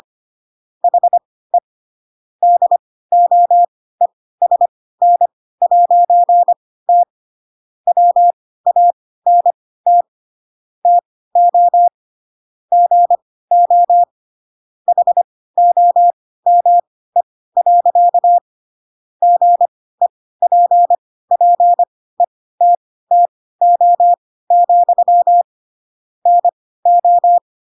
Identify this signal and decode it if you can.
Morse code, heard as a keyed tone, about 25 wpm, spaced as for 16 wpm, 700 Hz. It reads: HE DOESN'T WANT TO GO HOME. GEPPETTO, NO
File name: Pinocchio / Pinocchio ch3 10 25 ik1hge